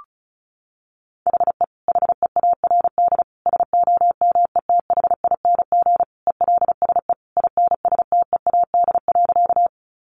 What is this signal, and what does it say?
Morse code, heard as a keyed tone, about 34 wpm, 700 Hz. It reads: HE HEARD SOMETHING ELSE INSTEAD.
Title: Sentences from Top 800 Words 35wpm